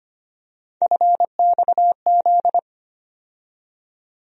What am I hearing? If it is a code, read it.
Morse code, heard as a keyed tone, about 25 wpm, 700 Hz. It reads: FXZ
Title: USA Call Sign Postfixes - 3 Letters 25wpm